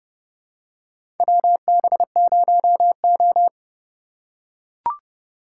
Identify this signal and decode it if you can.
Morse code, heard as a keyed tone, about 30 wpm, 700 Hz. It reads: WB0O
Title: USA Call Signs - 8x Character Spacing - 30wpm